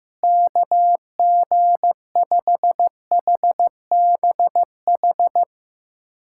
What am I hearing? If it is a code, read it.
Morse code, heard as a keyed tone, about 15 wpm, 700 Hz. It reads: KG5HBH
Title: USA Callsign DE Callsign - 6 Characters - 15wpm